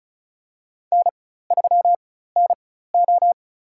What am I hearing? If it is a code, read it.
Morse code, heard as a keyed tone, about 36 wpm, spaced as for 16 wpm, 700 Hz. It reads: N3NO
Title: USA Call Signs - 4 Characters - 4x Character Spacing - 35wpm